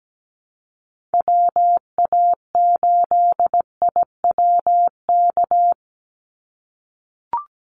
Morse code, heard as a keyed tone, about 17 wpm, 700 Hz. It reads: WA8IWK
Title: Call Signs - 17wpm